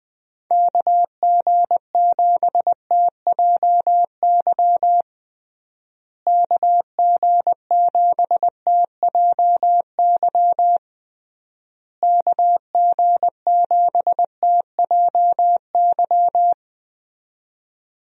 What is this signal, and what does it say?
Morse code, heard as a keyed tone, about 20 wpm, 700 Hz. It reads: KG7TJY KG7TJY KG7TJY
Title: USA Call Signs - 6 Characters - Repeated 3x - 20wpm